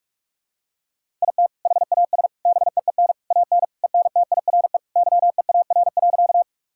Morse code, heard as a keyed tone, about 41 wpm, 700 Hz. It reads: IT HAS BEEN AN ENTIRE YEAR.
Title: Sentences from Top 1000 Words - Review Part 5 - 45wpm